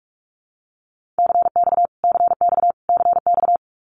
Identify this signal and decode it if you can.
Morse code, heard as a keyed tone, about 44 wpm, 700 Hz. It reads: CX CX CX
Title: Abbreviations Repeated 3x - 45wpm